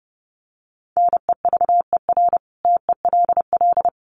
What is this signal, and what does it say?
Morse code, heard as a keyed tone, about 30 wpm, 700 Hz. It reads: NEVER TELL
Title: Sentences from Top 900 Words 30wpm